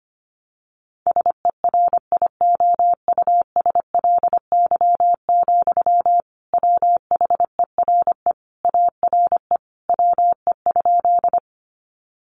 Morse code, heard as a keyed tone, about 25 wpm, 700 Hz. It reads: SERIOUSLY, WHERE ARE WE?
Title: Sentences from Top 1200 Words 25wpm